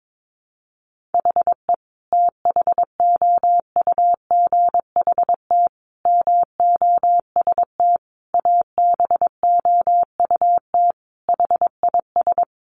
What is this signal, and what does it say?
Morse code, heard as a keyed tone, about 22 wpm, 700 Hz. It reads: HE THOUGHT MOST ABOUT HIS
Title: Sentences from Top 1500 Words 22wpm